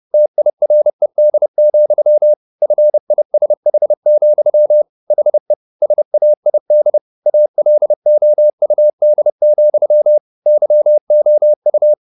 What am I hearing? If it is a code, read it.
Morse code, heard as a keyed tone, about 30 wpm, 600 Hz. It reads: TIRED, FISH, HE SAID ALOUD, YOU
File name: the old man and the sea / the old man and the sea 1 30wpm